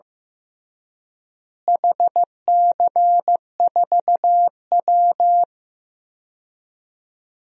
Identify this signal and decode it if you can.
Morse code, heard as a keyed tone, about 15 wpm, 700 Hz. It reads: HC4W
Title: DXCC-100 Call Signs - 4 Characters - Repeated 3x - 15wpm